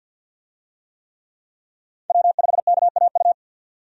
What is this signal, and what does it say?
Morse code, heard as a keyed tone, about 53 wpm, 700 Hz. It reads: W5KAU
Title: USA Call Signs - 5 Characters - 50wpm